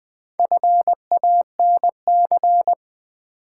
Morse code, heard as a keyed tone, about 20 wpm, 700 Hz. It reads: FANC
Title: Build A Word - 1 Letter At A Time - 3 to 5-Letter Words - 20wpm